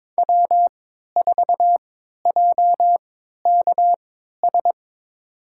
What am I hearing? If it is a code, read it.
Morse code, heard as a keyed tone, about 22 wpm, spaced as for 13 wpm, 700 Hz. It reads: W4JKS
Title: USA Call Signs - 5 Characters - 3x Character Spacing - 22wpm